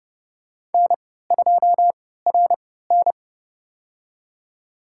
Morse code, heard as a keyed tone, about 32 wpm, spaced as for 17 wpm, 700 Hz. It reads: N2RN
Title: USA Call Signs - 4 Characters - 3x Character Spacing - 30wpm